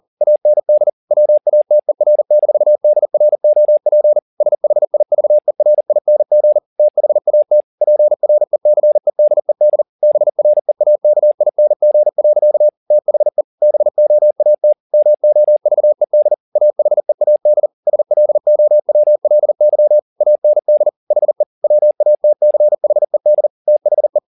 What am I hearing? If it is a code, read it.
Morse code, heard as a keyed tone, about 40 wpm, 600 Hz. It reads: AND WATER-DROP SHIVERING THAT PRECEDED BREAKING. THE BOAT MOVED AHEAD SLOWLY AND HE WATCHED THE
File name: the old man and the sea 2 40wpm